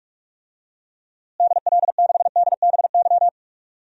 Morse code, heard as a keyed tone, about 46 wpm, 700 Hz. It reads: DL6DBY
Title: DXCC-100 Call Signs - 6 Characters - Repeated 3x - 45wpm